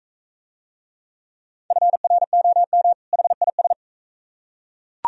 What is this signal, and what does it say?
Morse code, heard as a keyed tone, about 40 wpm, 700 Hz. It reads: FROM HIS